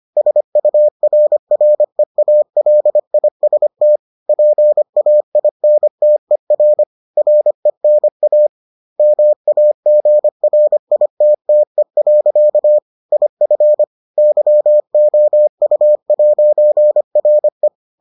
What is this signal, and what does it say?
Morse code, heard as a keyed tone, about 25 wpm, 600 Hz. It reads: SURREALIST PAINTER RENA MAGRITTE. IF YOU'RE